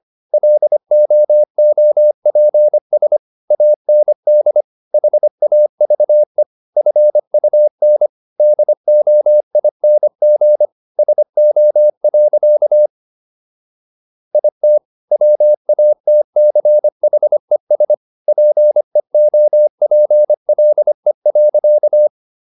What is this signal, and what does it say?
Morse code, heard as a keyed tone, about 25 wpm, 600 Hz. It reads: LOOPS AND HAVE FUN DOING SO. IT WATCHES PEOPLE.